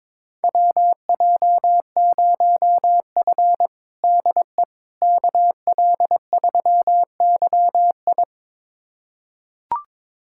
Morse code, heard as a keyed tone, about 22 wpm, 700 Hz. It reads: WJ0F DE KL3YI
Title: USA Callsign DE Callsign 22wpm